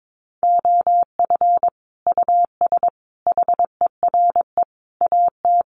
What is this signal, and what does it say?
Morse code, heard as a keyed tone, about 22 wpm, 700 Hz. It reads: OF US HERE AT
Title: Sets of 5 Words 22wpm